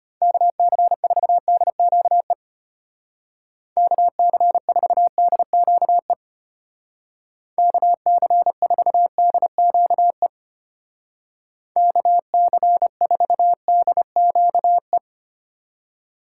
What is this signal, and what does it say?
Morse code, heard as a keyed tone, about 33 wpm, 700 Hz. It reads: KC4DQE KC4DQE KC4DQE KC4DQE